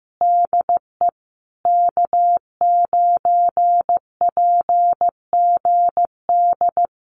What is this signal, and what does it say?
Morse code, heard as a keyed tone, about 15 wpm, 700 Hz. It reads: DE K9PGD
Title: Common QSO Phrases Encore 15wpm